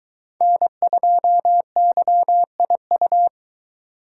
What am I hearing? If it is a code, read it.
Morse code, heard as a keyed tone, about 23 wpm, 700 Hz. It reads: N2YIU